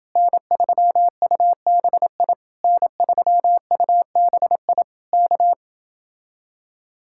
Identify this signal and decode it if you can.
Morse code, heard as a keyed tone, about 30 wpm, 700 Hz. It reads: N3UBI N3UBI K